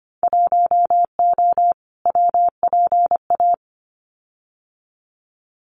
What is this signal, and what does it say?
Morse code, heard as a keyed tone, about 25 wpm, 700 Hz. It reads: 1O WPA
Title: Winter Field Day Exchanges 25wpm